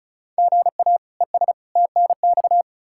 Morse code, heard as a keyed tone, about 34 wpm, 700 Hz. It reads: GA ES TNX
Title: Practice QSO - Exchange 1 - Beginning - 35wpm